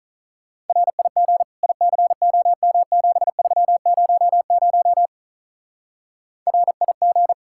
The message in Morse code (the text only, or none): RIG ICOM7300 RIG